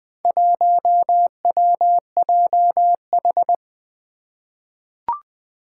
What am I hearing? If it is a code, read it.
Morse code, heard as a keyed tone, about 20 wpm, 700 Hz. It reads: 1WJH
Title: Four Challenging Letter-Number Combinations 20wpm